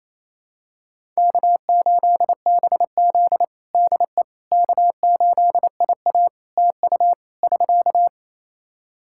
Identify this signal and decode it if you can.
Morse code, heard as a keyed tone, about 28 wpm, 700 Hz. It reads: K8BZ DE K8IA TU <SK>
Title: K1USN SST QSO Practice 28wpm